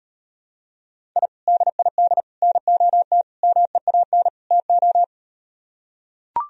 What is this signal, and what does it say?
Morse code, heard as a keyed tone, about 38 wpm, 700 Hz. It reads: I DID NOT MEAN TO